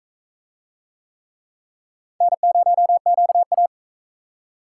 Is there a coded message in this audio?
Yes: Morse code from 2.2 to 3.7 s, heard as a keyed tone, about 40 wpm, 700 Hz. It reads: N0QA